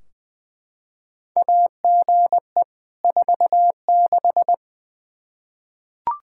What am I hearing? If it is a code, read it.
Morse code, heard as a keyed tone, about 21 wpm, 700 Hz. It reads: AGE 46
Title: Practice QSO - Exchange 3 - More Info - 20wpm